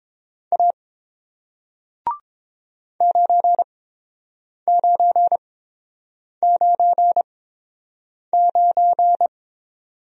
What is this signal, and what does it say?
Morse code, heard as a keyed tone, about 30 wpm, spaced as for 7 wpm, 700 Hz. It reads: A 9999